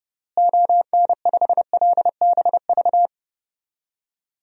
Morse code, heard as a keyed tone, about 30 wpm, 700 Hz. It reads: ON5LBV
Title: DXCC-100 Call Signs - 6 Characters - Repeated 2x - 30wpm